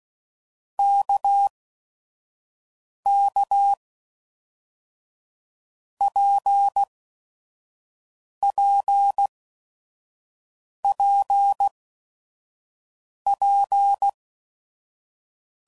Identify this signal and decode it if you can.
Morse code, heard as a keyed tone, about 16 wpm, 800 Hz. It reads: K K P P P P
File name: Basic Letter Groups By7Groups / RKPX / RKPX@5 f15